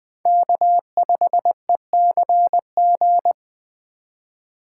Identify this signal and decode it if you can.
Morse code, heard as a keyed tone, about 20 wpm, 700 Hz. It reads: K5ECG
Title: USA Call Signs - 5 Characters - 3x Character Spacing - 20wpm